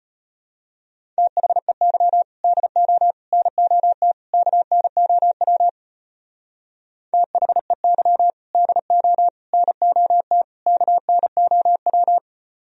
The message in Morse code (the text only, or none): THEY DO NOT KNOW THEY DO NOT KNOW